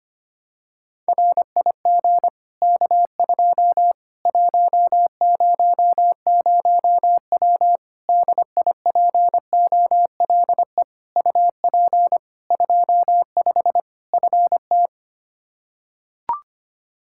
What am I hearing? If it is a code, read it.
Morse code, heard as a keyed tone, about 25 wpm, 700 Hz. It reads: RIG K2 100W DIPOLE UP 25 FT
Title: Practice QSO - Exchange 2 - Info - 25wpm